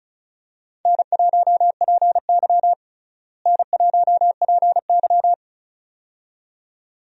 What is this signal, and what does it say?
Morse code, heard as a keyed tone, about 36 wpm, 700 Hz. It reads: N1PY N1PY
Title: USA Call Signs - 4 Characters - Repeated 2x - 35wpm